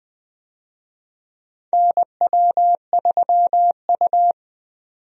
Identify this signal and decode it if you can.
Morse code, heard as a keyed tone, about 20 wpm, 700 Hz. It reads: NW3U
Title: USA Call Signs - 4 Characters - 20wpm